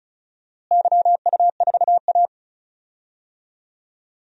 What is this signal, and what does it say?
Morse code, heard as a keyed tone, about 34 wpm, 700 Hz. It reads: YU4A
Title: DXCC-100 Call Signs - 4 Characters - Repeated 3x - 35wpm